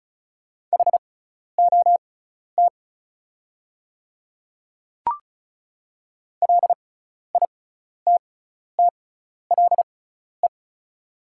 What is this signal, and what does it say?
Morse code, heard as a keyed tone, about 36 wpm, spaced as for 12 wpm, 690 Hz. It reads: HOT LITTLE
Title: 6x Character Spacing - Top 500 Words - 35wpm